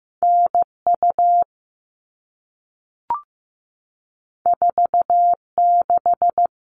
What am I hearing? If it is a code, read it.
Morse code, heard as a keyed tone, about 15 wpm, 700 Hz. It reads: NU 46
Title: All Cut Numbers - 2-Digit 15wpm